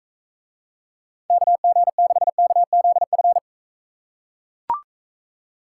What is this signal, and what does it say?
Morse code, heard as a keyed tone, about 41 wpm, 700 Hz. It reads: KG6KZF